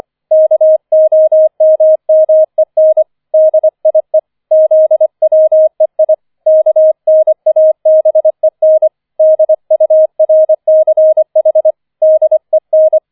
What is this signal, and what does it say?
Morse code, heard as a keyed tone, about 25 wpm, 630 Hz. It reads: KOMMEN DIE ZWEI KNABEN DURCH DEN